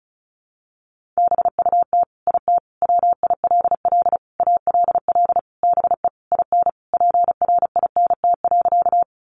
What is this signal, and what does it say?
Morse code, heard as a keyed tone, about 34 wpm, 700 Hz. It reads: BUT IT WILL ALL BE IN PRINT.
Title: Sentences from Top 1400 Words 35wpm